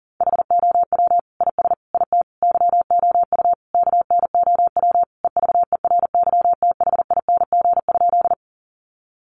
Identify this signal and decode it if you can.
Morse code, heard as a keyed tone, about 40 wpm, 700 Hz. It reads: HOW IS IT YOU KNOW EVERYTHING?
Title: Sentences from Top 700 Words 40wpm